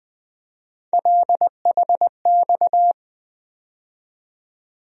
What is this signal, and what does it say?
Morse code, heard as a keyed tone, about 20 wpm, 700 Hz. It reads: LHX
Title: USA Call Sign Postfixes - 3 Letters 20wpm